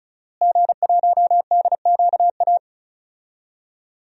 Morse code, heard as a keyed tone, about 34 wpm, 700 Hz. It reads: G1DQA
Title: DXCC-100 Call Signs - 5 Characters - Repeated 2x - 35wpm